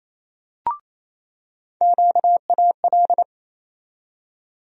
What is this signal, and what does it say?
Morse code, heard as a keyed tone, about 28 wpm, 700 Hz. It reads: QAL
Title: USA Call Sign Postfixes - 3 Letters 28wpm